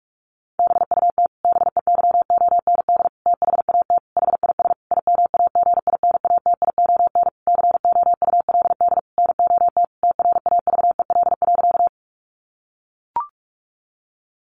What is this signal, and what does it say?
Morse code, heard as a keyed tone, about 44 wpm, 700 Hz. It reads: BUT BEYOND THAT HIS IMAGINATION COULD NOT TRAVEL.
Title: Sentences from Top 1300 Words 45wpm